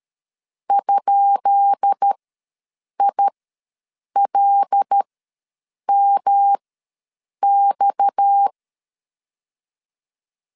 Morse code, heard as a keyed tone, about 13 wpm, spaced as for 7 wpm, 800 Hz. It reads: ?ILMX
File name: LEZIONI / 10 SIGNS / 10B INT ws 12wpm 3527